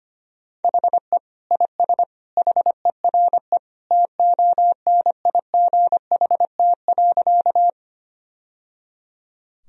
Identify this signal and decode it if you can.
Morse code, heard as a keyed tone, about 25 wpm, 700 Hz. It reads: HE IS HERE TONIGHT.
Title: Sentences from Top 1000 Words - Review Part 3 - 25wpm